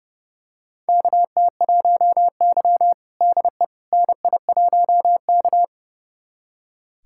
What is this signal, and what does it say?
Morse code, heard as a keyed tone, about 30 wpm, 700 Hz. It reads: KT1Y DE NI1K
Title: USA Callsign DE Callsign - 4 Characters - 30wpm